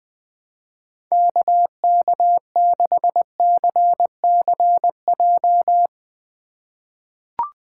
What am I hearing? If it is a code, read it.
Morse code, heard as a keyed tone, about 21 wpm, 700 Hz. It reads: KK6CCJ
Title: USA Call Signs - 6 Characters - 3x Character Spacing - 20wpm